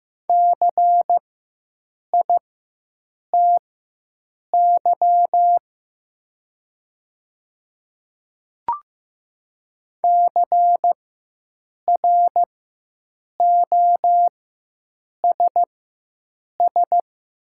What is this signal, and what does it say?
Morse code, heard as a keyed tone, about 15 wpm, spaced as for 7 wpm, 700 Hz. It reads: CITY CROSS
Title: 4x Character Spacing - Top 300 Words - 15wpm